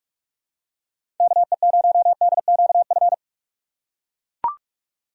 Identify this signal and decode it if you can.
Morse code, heard as a keyed tone, about 43 wpm, 700 Hz. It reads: KE0DQF
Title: USA Call Signs - 6 Characters - 4x Character Spacing - 45wpm